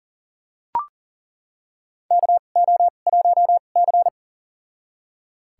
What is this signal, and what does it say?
Morse code, heard as a keyed tone, about 40 wpm, spaced as for 29 wpm, 700 Hz. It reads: KO1C
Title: USA Call Signs - 4 Characters - 2x Character Spacing - 40wpm